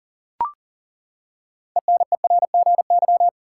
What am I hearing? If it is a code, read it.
Morse code, heard as a keyed tone, about 40 wpm, 700 Hz. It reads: ENERGY
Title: Six Letter Words - 40wpm